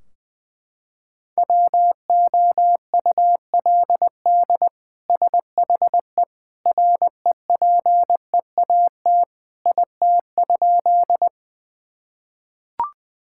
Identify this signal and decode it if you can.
Morse code, heard as a keyed tone, about 20 wpm, 700 Hz. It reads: WOULD SHE REPEAT IT?
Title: Sentences from Top 1600 Words 20wpm